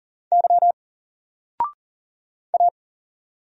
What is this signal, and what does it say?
Morse code, heard as a keyed tone, about 40 wpm, 700 Hz. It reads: Y A
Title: Single Letter-Number 40wpm